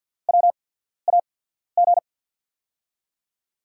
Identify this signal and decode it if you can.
Morse code, heard as a keyed tone, about 51 wpm, spaced as for 14 wpm, 690 Hz. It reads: WAG
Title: USA Call Signs - 6 Characters - 8x Character Spacing - 50wpm